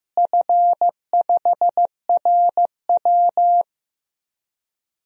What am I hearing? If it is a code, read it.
Morse code, heard as a keyed tone, about 15 wpm, 700 Hz. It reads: F5RW
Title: DXCC-100 Call Signs - 4 Characters - Repeated 3x - 15wpm